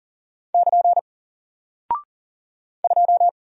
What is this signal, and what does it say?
Morse code, heard as a keyed tone, about 40 wpm, 700 Hz. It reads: <KN> 2
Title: Single Character (Letter:Number:Punctuation:Prosign) 40wpm